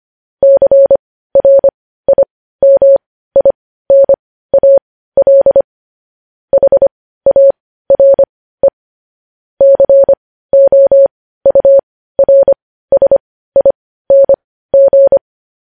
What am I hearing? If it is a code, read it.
Morse code, heard as a keyed tone, about 25 wpm, spaced as for 15 wpm, 550 Hz. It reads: CRIMINAL HARE COURSING
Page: news-headlines-in-morse-code-at-15-wpm